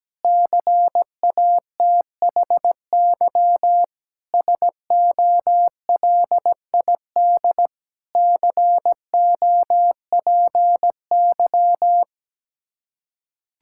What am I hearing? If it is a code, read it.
Morse code, heard as a keyed tone, about 17 wpm, 700 Hz. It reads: CATHY SOLID COPY
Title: K1USN SST QSO Practice 17wpm